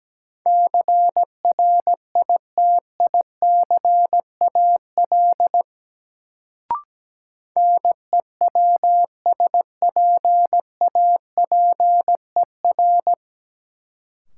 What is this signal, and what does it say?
Morse code, heard as a keyed tone, about 17 wpm, 700 Hz. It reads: CRITICAL NEWSPAPER
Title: Sentences from Top 1100 Words - New Word Drill Alt - 17wpm